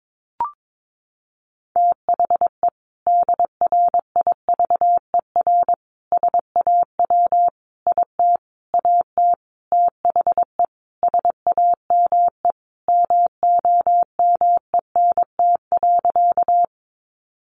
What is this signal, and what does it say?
Morse code, heard as a keyed tone, about 22 wpm, 700 Hz. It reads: THE DRIVER SAW IT AT THE SAME MOMENT.
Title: Sentences from Top 1000 Words 22wpm